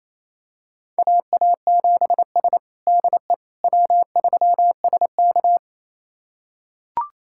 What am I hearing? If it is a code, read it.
Morse code, heard as a keyed tone, about 27 wpm, 700 Hz. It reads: AA7S DE W3SK
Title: USA Callsign DE Callsign - 4 Characters - 28wpm